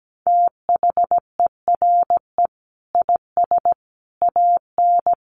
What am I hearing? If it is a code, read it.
Morse code, heard as a keyed tone, about 17 wpm, 700 Hz. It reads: THERE IS AN
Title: Sentences from Top 1400 Words 17wpm